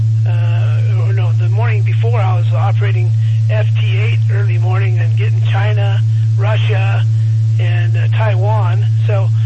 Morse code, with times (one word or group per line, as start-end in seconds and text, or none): none